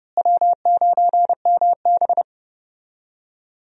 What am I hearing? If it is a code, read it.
Morse code, heard as a keyed tone, about 30 wpm, 700 Hz. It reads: W9MB
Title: USA Call Signs - 4 Characters - Repeated 3x - 30wpm